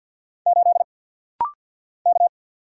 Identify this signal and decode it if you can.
Morse code, heard as a keyed tone, about 50 wpm, 700 Hz. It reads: 8 K
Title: Single Letter-Number Alt - Rapid-Fire 50wpm